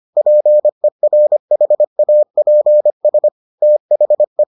Morse code, heard as a keyed tone, about 25 wpm, 600 Hz. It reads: PERHAPS THE